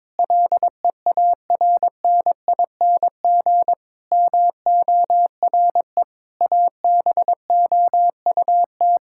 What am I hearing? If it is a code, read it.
Morse code, heard as a keyed tone, about 22 wpm, 700 Hz. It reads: LEARNING MORE ABOUT
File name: practice / 22wpm / Sets of 8 Words - 22wpm